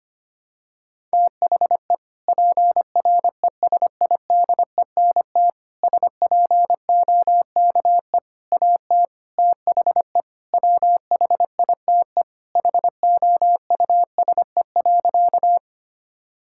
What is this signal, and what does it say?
Morse code, heard as a keyed tone, about 25 wpm, 700 Hz. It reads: THE PRESIDENT SPOKE AT THE WHITE HOUSE.
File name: practice / 25wpm / Sentences with 4-Word Phrases - 25wpm